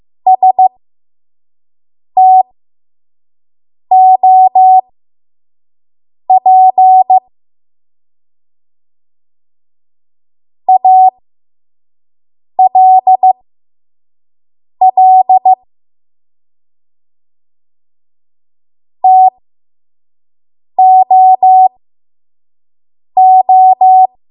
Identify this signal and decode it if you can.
Morse code, heard as a keyed tone, about 15 wpm, spaced as for 5 wpm, 750 Hz. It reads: STOP ALL TOO
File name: Archive / 5 WPM / 05WPM